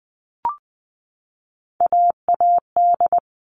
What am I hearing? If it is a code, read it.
Morse code, heard as a keyed tone, about 20 wpm, 700 Hz. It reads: AAD
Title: All Cut Numbers - 3-Digit 20wpm